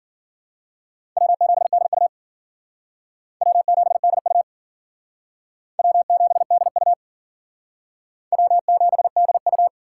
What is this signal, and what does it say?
Morse code, heard as a keyed tone, about 50 wpm, 700 Hz. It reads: W7DU W7DU W7DU W7DU
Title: Speed Racing - USA Call Signs 40wpm